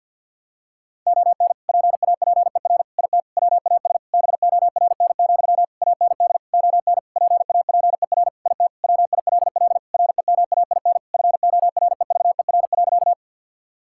Morse code, heard as a keyed tone, about 50 wpm, 700 Hz. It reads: ON PAPER IT WAS BORN, AND ON PAPER IT WILL REMAIN FOREVER.
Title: Sentences from Top 1600 Words 50wpm